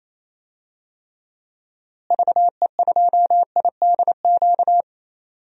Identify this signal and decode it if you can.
Morse code, heard as a keyed tone, about 28 wpm, 700 Hz. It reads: VE2IDQ
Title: DXCC-100 Call Signs - 6 Characters - Repeated 2x - 28wpm